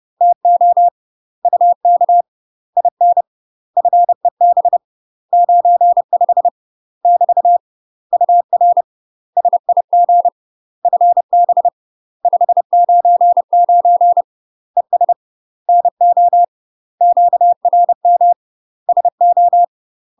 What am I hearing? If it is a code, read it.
Morse code, heard as a keyed tone, about 30 wpm, 700 Hz. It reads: TO UK IN FEB 95 <BT> UR SIG FB 599 ES NO QRM SO
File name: UK MASTER QSOs TRAINING / 30WPM / UK MASTER QSOs 0020